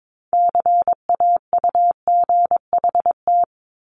Morse code, heard as a keyed tone, about 22 wpm, 700 Hz. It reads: CAUGHT
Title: Multi-letter Phonemes Repeated 3x 22wpm